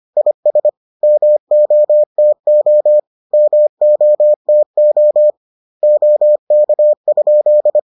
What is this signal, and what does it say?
Morse code, heard as a keyed tone, about 25 wpm, 600 Hz. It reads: IS MOTO MOTO OK?